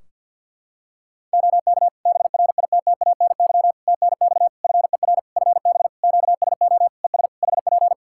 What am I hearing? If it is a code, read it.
Morse code, heard as a keyed tone, about 51 wpm, 700 Hz. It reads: OK BRITTANY TNX FER FB QSO ES HP
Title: K1USN SST QSO Practice 50wpm